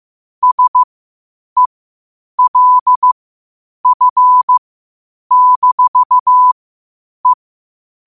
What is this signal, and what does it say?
Morse code, heard as a keyed tone, about 15 wpm, spaced as for 8 wpm, 1000 Hz. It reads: SELF-E
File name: qotdfiles / 8wpm